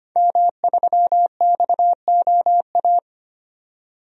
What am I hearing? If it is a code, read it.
Morse code, heard as a keyed tone, about 25 wpm, 700 Hz. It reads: M3XOA
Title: DXCC-100 Call Signs - 5 Characters - 25wpm